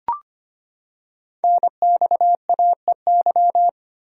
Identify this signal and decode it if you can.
Morse code, heard as a keyed tone, about 25 wpm, 700 Hz. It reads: NXAEY
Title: Code Groups - 5 Letters - 25wpm